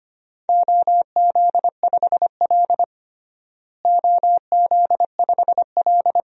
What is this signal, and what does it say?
Morse code, heard as a keyed tone, about 25 wpm, 700 Hz. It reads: OZ5L OZ5L
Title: DXCC-100 Call Signs - 4 Characters - Repeated 3x - 25wpm